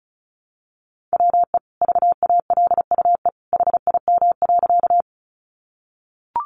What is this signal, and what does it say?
Morse code, heard as a keyed tone, about 34 wpm, 700 Hz. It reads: WE VALUE HIM.
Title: Sentences from Top 600 Words 35wpm